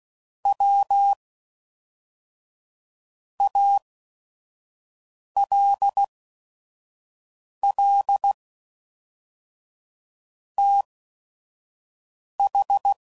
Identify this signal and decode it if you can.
Morse code, heard as a keyed tone, about 16 wpm, spaced as for 4 wpm, 800 Hz. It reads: WALL TH
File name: CW Code Groups @5WPM / QSOs / QSO WD0FBT